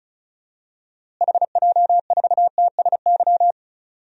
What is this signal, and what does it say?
Morse code, heard as a keyed tone, about 34 wpm, 700 Hz. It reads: HJ4TSY
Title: DXCC-100 Call Signs - 6 Characters - Repeated 3x - 35wpm